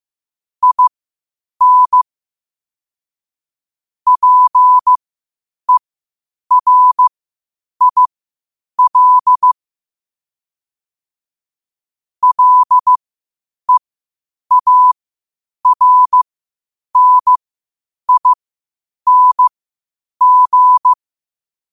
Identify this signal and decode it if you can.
Morse code, heard as a keyed tone, about 15 wpm, spaced as for 8 wpm, 1000 Hz. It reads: IN PERIL LEARNING
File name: qotdfiles / 8wpm